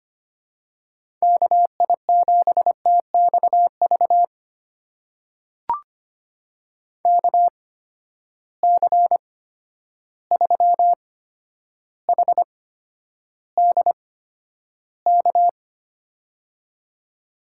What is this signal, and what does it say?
Morse code, heard as a keyed tone, about 25 wpm, spaced as for 11 wpm, 700 Hz. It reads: KI7TXV KC3HDK
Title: USA Call Signs - 6 Characters - 8x Character Spacing - 25wpm